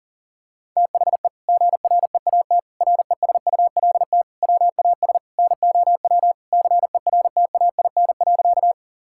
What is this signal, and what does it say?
Morse code, heard as a keyed tone, about 37 wpm, 700 Hz. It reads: THE GREAT RESULT WAS NOW CERTAIN.
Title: Sentences from Top 1000 Words - Review Part 5 - 40wpm